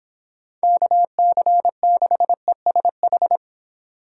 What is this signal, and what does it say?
Morse code, heard as a keyed tone, about 29 wpm, 700 Hz. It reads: KC6ESH